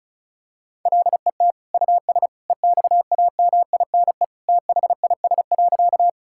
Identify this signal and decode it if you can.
Morse code, heard as a keyed tone, about 34 wpm, 700 Hz. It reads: LET US EXAMINE THIS.
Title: Sentences from Top 1600 Words 35wpm